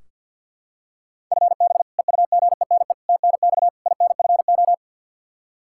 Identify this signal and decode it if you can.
Morse code, heard as a keyed tone, about 45 wpm, 700 Hz. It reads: FB EUGENE TNX INFO
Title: Practice QSO - Exchange 3 - More Info - 50wpm